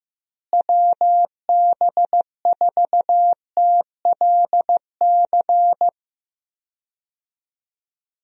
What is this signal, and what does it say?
Morse code, heard as a keyed tone, about 15 wpm, 700 Hz. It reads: WB4TLC
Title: USA Call Signs - 6 Characters - Repeated 2x - 15wpm